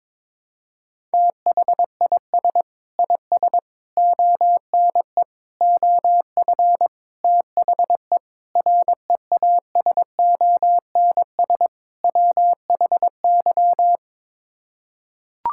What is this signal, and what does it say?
Morse code, heard as a keyed tone, about 22 wpm, 700 Hz. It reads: THIS IS ONE OF THE REASONS WHY
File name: practice / 22wpm / Sets of 7 Words - 22wpm